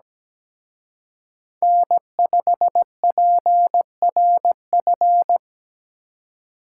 Morse code, heard as a keyed tone, about 20 wpm, 700 Hz. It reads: N5PRF
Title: Speed Racing - USA 1x3 Call Signs 17wpm